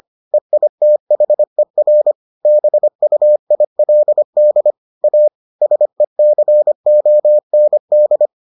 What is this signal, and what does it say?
Morse code, heard as a keyed tone, about 25 wpm, 600 Hz. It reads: EITHER BUILD A SECOND